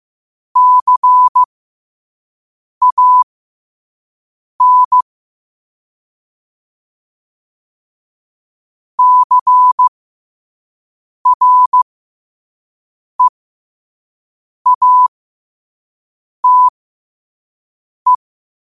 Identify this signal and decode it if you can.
Morse code, heard as a keyed tone, about 15 wpm, spaced as for 5 wpm, 1000 Hz. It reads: CAN CREATE
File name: qotdfiles / 5wpm